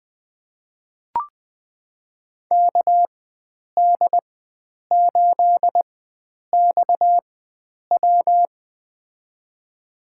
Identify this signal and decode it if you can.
Morse code, heard as a keyed tone, about 20 wpm, spaced as for 9 wpm, 700 Hz. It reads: KD8XW
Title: USA Call Signs - 5 Characters - 4x Character Spacing - 20wpm